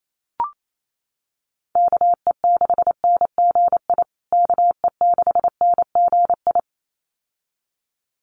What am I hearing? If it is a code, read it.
Morse code, heard as a keyed tone, about 28 wpm, 700 Hz. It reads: KE6NGI KE6NGI
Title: USA Call Signs Repeated 2x 28wpm